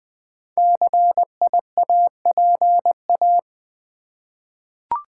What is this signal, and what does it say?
Morse code, heard as a keyed tone, about 20 wpm, 700 Hz. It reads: CIAPA
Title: Code Groups - 5 Letters - 20wpm